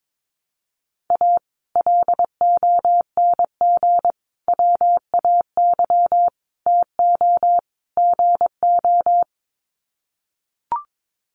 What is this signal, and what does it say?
Morse code, heard as a keyed tone, about 22 wpm, 700 Hz. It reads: A LONG WAY TO GO
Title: Sets of 5 Words 22wpm